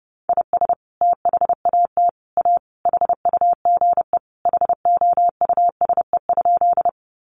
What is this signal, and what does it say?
Morse code, heard as a keyed tone, about 30 wpm, 700 Hz. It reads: IS THAT A HUGE HOUSE?
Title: Sentences from Top 1100 Words 30wpm